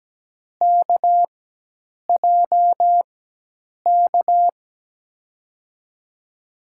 Morse code, heard as a keyed tone, about 17 wpm, spaced as for 8 wpm, 700 Hz. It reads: KJK
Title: USA Call Signs - 6 Characters - 4x Character Spacing - 17wpm